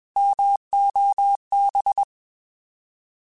Morse code, heard as a keyed tone, about 21 wpm, 800 Hz. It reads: MOB
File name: Alphabetical / Alphabetical mp3s / Ms @20